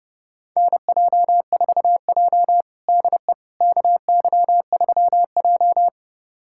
Morse code, heard as a keyed tone, about 30 wpm, 700 Hz. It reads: NJ4J DE KY3J
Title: USA Callsign DE Callsign - 4 Characters - 30wpm